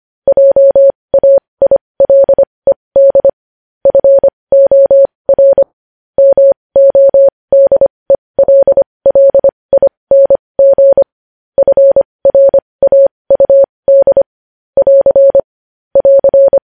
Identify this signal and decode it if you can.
Morse code, heard as a keyed tone, about 25 wpm, spaced as for 20 wpm, 550 Hz. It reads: JAILED FOR MODELLING FRAUD <AR> <AR>
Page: News Headlines in Morse Code at 20 WPM